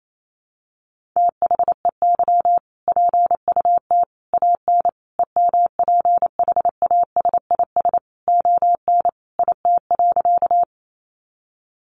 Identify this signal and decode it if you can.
Morse code, heard as a keyed tone, about 28 wpm, 700 Hz. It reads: THEY PUT AN EMPHASIS ON IT.
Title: Sentences from Top 1400 Words 28wpm